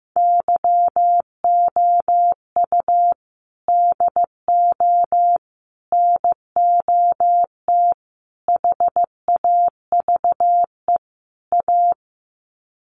Morse code, heard as a keyed tone, about 15 wpm, 700 Hz. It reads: YOU DO NOT HAVE A
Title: Sets of 5 Words 15wpm